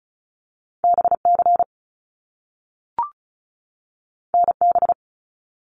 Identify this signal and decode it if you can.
Morse code, heard as a keyed tone, about 36 wpm, 700 Hz. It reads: BC NB
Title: Canadian Provinces and Territories 35wpm